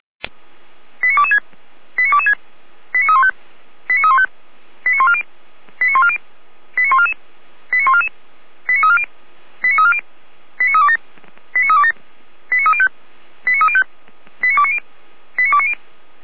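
Five-tone selective call (ZVEI), heard as quick runs of five tones, tones of about 80 ms. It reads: 89226 89226 89315 89315 89140 89140 89144 89144 89340 89340 89327 89327 89305 89305 89290 89290
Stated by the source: ZVEI five-tone selective call 89167 received on SDR by feuerwehr-orlamuende (fire brigade Orlamünde).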